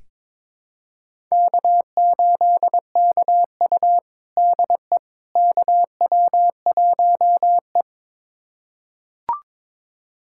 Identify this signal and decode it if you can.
Morse code, heard as a keyed tone, about 21 wpm, 700 Hz. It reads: K8KU DE KW1E
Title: USA Callsign DE Callsign - 4 Characters - 22wpm